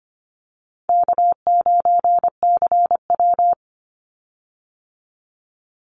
Morse code, heard as a keyed tone, about 25 wpm, 700 Hz. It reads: K9CW
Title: Easy Call Signs - 25wpm